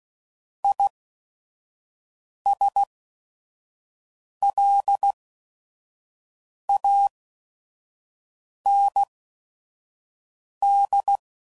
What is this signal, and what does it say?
Morse code, heard as a keyed tone, about 16 wpm, spaced as for 5 wpm, 800 Hz. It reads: ISLAND
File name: Prefixes / By First Letter / Is